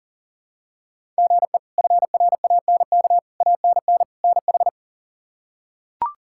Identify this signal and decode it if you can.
Morse code, heard as a keyed tone, about 40 wpm, 700 Hz. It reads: GE FRANK ANN NH
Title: K1USN SST QSO Practice - 40wpm